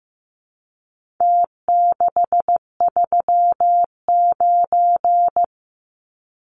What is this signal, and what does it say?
Morse code, heard as a keyed tone, about 15 wpm, 700 Hz. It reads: T639
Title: Numbers - Contest Style - 4-Digit 15wpm